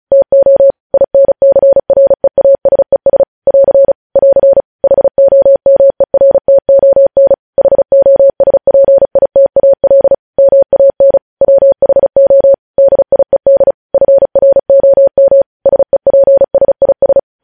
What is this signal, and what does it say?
Morse code, heard as a keyed tone, about 36 wpm, 550 Hz. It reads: TO INCREASES <AR> <AR> HOMERTON HOSPITAL MAN WHO DIED FROM SEPSIS
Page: News headlines in Morse code at 35 wpm